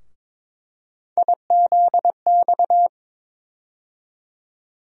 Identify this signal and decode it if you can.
Morse code, heard as a keyed tone, about 22 wpm, 700 Hz. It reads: IZX
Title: USA Call Sign Postfixes - 3 Letters 22wpm